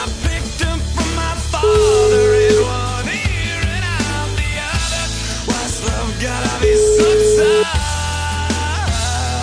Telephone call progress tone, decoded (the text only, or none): ringback tone